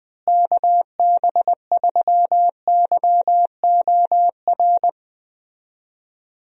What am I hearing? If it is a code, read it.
Morse code, heard as a keyed tone, about 20 wpm, 700 Hz. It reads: KB3YOR
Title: USA Call Signs - 6 Characters - Repeated 2x - 20wpm